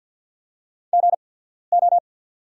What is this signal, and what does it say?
Morse code, heard as a keyed tone, about 49 wpm, spaced as for 14 wpm, 690 Hz. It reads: GO
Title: 8x Character Spacing - Top 100 Words - 50wpm